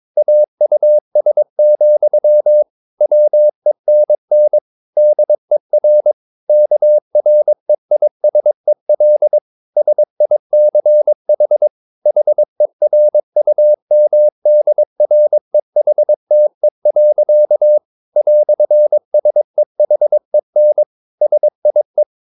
Morse code, heard as a keyed tone, about 22 wpm, 600 Hz. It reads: AUS, WENN DER KREISEL SICH HERUMDREHTE. "SEHEN SIE